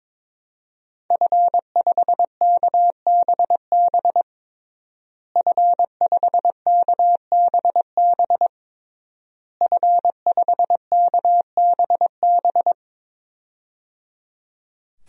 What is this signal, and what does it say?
Morse code, heard as a keyed tone, about 22 wpm, 700 Hz. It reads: F5KBB F5KBB F5KBB
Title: DXCC-100 Call Signs - 5 Characters - Repeated 3x - 22wpm